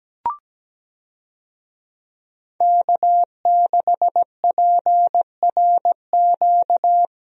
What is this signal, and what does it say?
Morse code, heard as a keyed tone, about 17 wpm, 700 Hz. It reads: K6PRQ
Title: USA Call Signs - 5 Characters - 17wpm